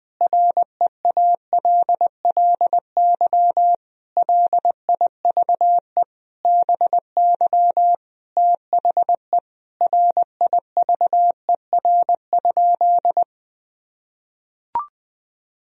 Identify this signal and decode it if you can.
Morse code, heard as a keyed tone, about 20 wpm, 700 Hz. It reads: REALLY LIVE BY THE RIVER?
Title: Sentences from Top 1000 Words - Review Part 1 - 20wpm